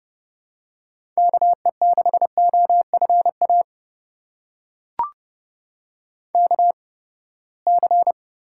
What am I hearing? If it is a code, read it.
Morse code, heard as a keyed tone, about 32 wpm, spaced as for 21 wpm, 700 Hz. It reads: KE6OFA KC
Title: USA Call Signs - 6 Characters - 8x Character Spacing - 30wpm